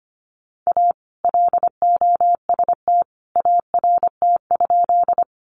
Morse code, heard as a keyed tone, about 25 wpm, 700 Hz. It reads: A LOST ART?
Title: Sentences from Top 1400 Words 25wpm